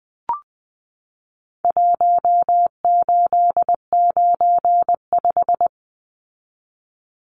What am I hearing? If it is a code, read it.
Morse code, heard as a keyed tone, about 20 wpm, 700 Hz. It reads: 1895
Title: Numbers - Contest Style - 4-Digit 20wpm